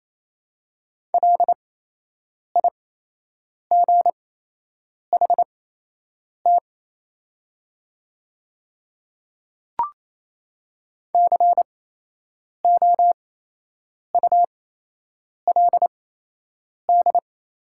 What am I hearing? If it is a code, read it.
Morse code, heard as a keyed tone, about 28 wpm, spaced as for 8 wpm, 700 Hz. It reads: LIGHT COULD
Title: 8x Character Spacing - Top 300 Words - 28wpm